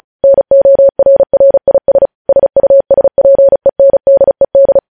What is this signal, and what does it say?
Morse code, heard as a keyed tone, about 36 wpm, 550 Hz. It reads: NORRIS SUSPENDED
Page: News Headlines in Morse Code at 35 WPM